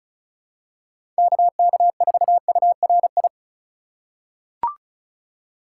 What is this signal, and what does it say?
Morse code, heard as a keyed tone, about 36 wpm, 700 Hz. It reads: KK4URI
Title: USA Call Signs - 6 Characters - 4x Character Spacing - 35wpm